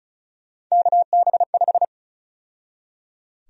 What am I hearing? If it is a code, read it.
Morse code, heard as a keyed tone, about 36 wpm, 700 Hz. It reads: KB5
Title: USA Call Sign Prefixes - 35wpm